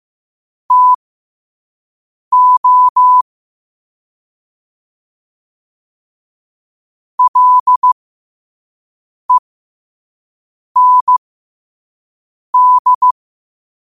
Morse code, heard as a keyed tone, about 15 wpm, spaced as for 5 wpm, 1000 Hz. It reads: TO LEND
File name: qotdfiles / 5wpm